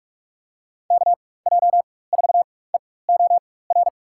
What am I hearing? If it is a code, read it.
Morse code, heard as a keyed tone, about 45 wpm, spaced as for 21 wpm, 700 Hz. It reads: KJ4EOR
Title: USA Call Signs - 6 Characters - 4x Character Spacing - 45wpm